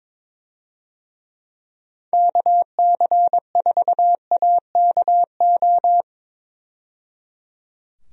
Morse code, heard as a keyed tone, about 22 wpm, 700 Hz. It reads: KC4AKO
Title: USA Call Signs - 6 Characters - 22wpm